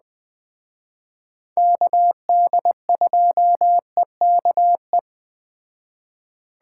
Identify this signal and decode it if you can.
Morse code, heard as a keyed tone, about 20 wpm, 700 Hz. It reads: KD2EKE